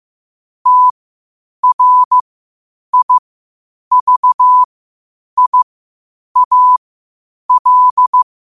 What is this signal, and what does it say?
Morse code, heard as a keyed tone, about 15 wpm, spaced as for 8 wpm, 1000 Hz. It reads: TRIVIAL
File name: qotdfiles / 8wpm